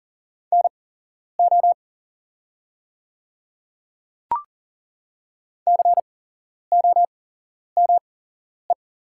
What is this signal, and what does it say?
Morse code, heard as a keyed tone, about 40 wpm, spaced as for 11 wpm, 700 Hz. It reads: NO COME
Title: 8x Character Spacing - Top 300 Words - 40wpm